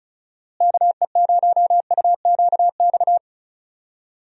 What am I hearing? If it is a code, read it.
Morse code, heard as a keyed tone, about 36 wpm, 700 Hz. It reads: KE0UQX
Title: USA Call Signs - 6 Characters - Repeated 3x - 35wpm